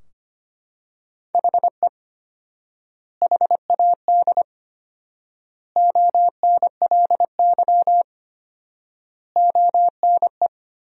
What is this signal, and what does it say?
Morse code, heard as a keyed tone, about 25 wpm, 700 Hz. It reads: HE HAD ONLY ONE
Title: 4x Word-Spacing - Sentences from Top 300 Words - 25wpm